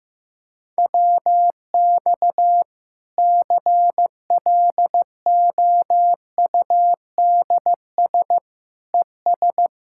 Morse code, heard as a keyed tone, about 15 wpm, 700 Hz. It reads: WX CLOUDS ES
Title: Practice QSO - Exchange 2 - Info - 15wpm